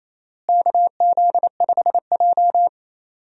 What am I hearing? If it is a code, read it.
Morse code, heard as a keyed tone, about 28 wpm, 700 Hz. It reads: KZ5J
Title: USA Call Signs - 4 Characters - Repeated 2x - 28wpm